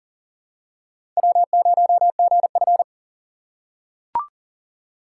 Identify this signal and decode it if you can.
Morse code, heard as a keyed tone, about 40 wpm, 700 Hz. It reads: W0GF
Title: USA Call Signs - 4 Characters - 4x Character Spacing - 40wpm